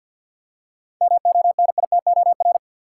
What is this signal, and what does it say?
Morse code, heard as a keyed tone, about 51 wpm, 700 Hz. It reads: MONITOR
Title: Seven Letter Words - 50wpm